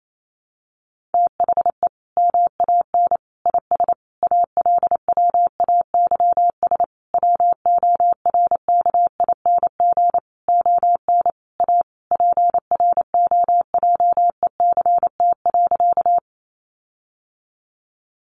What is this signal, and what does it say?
Morse code, heard as a keyed tone, about 28 wpm, 700 Hz. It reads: THE MAN IS ALWAYS WORKING ON A PROJECT.
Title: Sentences from Top 1400 Words 28wpm